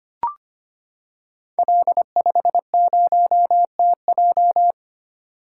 Morse code, heard as a keyed tone, about 25 wpm, 700 Hz. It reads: L50TJ
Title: Code Groups - 5 Letter:Numbers - 25wpm